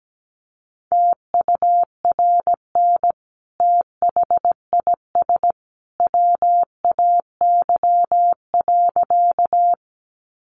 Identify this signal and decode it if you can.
Morse code, heard as a keyed tone, about 17 wpm, 700 Hz. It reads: TURN THIS WAY.
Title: Sentences from Top 200 Words 17wpm